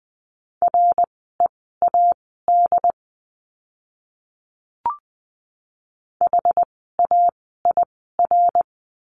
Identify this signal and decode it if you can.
Morse code, heard as a keyed tone, about 20 wpm, spaced as for 15 wpm, 700 Hz. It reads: READ HAIR
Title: 101 Lesson 17f - Letter u - Words 20wpm